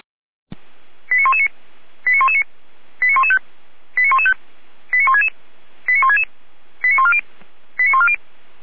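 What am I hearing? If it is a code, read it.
Five-tone selective call (ZVEI), heard as quick runs of five tones, tones of about 80 ms. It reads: 89119 89119 89115 89115 89166 89166 89140 89140
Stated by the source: ZVEI five-tone call 89073 received by SDR, logged by feuerwehr-orlamuende.org fire brigade server.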